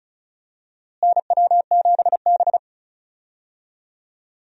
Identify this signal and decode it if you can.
Morse code, heard as a keyed tone, about 34 wpm, 700 Hz. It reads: NW7B
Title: USA Call Signs - 4 Characters - Repeated 2x - 35wpm